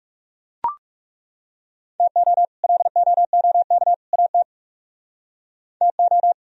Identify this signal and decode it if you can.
Morse code, heard as a keyed tone, about 37 wpm, 700 Hz. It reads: TO LOOK AT TO
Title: Speed-Racing - Sets of 3 Words 30wpm